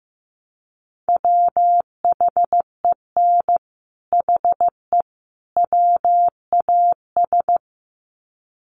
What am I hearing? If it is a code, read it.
Morse code, heard as a keyed tone, about 15 wpm, 700 Hz. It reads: WHEN HE WAS
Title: Sets of 3 Words 15wpm